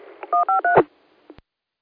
DTMF, heard as telephone touch tones, tones of about 130 ms. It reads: 153
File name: Old Forge 2022 11 20 13 13 25